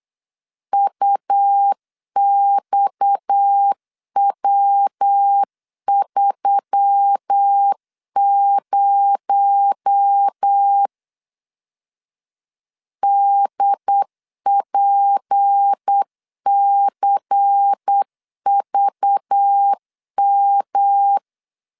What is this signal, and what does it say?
Morse code, heard as a keyed tone, about 8 wpm, 800 Hz. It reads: UXW30 DPCVM
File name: LEZIONI / 8 YJ45 / 8 Y 8wpm 3245